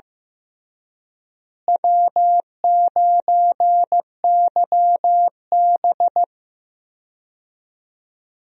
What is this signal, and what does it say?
Morse code, heard as a keyed tone, about 16 wpm, 700 Hz. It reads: W9YB
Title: USA Call Signs - 4 Characters - Repeated 2x - 15wpm